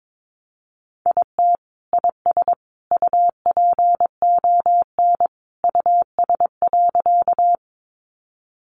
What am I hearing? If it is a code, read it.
Morse code, heard as a keyed tone, about 22 wpm, 700 Hz. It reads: IT IS UPON US.
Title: Sentences from Top 1100 Words 22wpm